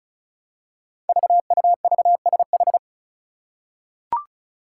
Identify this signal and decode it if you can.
Morse code, heard as a keyed tone, about 36 wpm, 700 Hz. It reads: VUVSH
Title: Five Challenging Letter-Number Combinations 35wpm